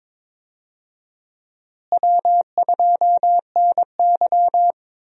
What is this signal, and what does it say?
Morse code, heard as a keyed tone, about 22 wpm, 700 Hz. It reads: W2NY
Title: USA Call Signs - 4 Characters - Repeated 3x - 22wpm